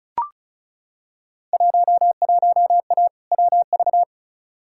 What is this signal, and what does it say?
Morse code, heard as a keyed tone, about 36 wpm, 700 Hz. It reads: 11A WV
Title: Field Day Exchanges Encore - 35wpm